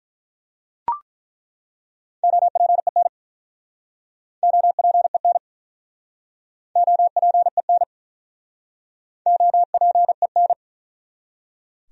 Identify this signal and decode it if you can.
Morse code, heard as a keyed tone, about 47 wpm, 700 Hz. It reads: OPEN OPEN OPEN OPEN